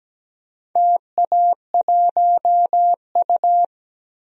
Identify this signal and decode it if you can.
Morse code, heard as a keyed tone, about 17 wpm, 700 Hz. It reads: TA1U
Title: DXCC-100 Call Signs - 4 Characters - 17wpm